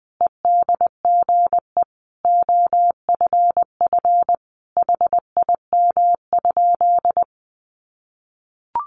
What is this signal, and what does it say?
Morse code, heard as a keyed tone, about 20 wpm, 700 Hz. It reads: EDGE OFF HIM?
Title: Sentences from Top 1300 Words 20wpm